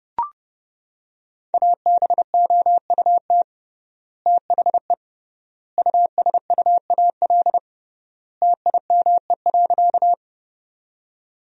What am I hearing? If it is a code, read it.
Morse code, heard as a keyed tone, about 30 wpm, 700 Hz. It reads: ABOUT THE USUAL TIME.
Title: 3x Word-Spacing - Sentences from Top 400 Words - 30wpm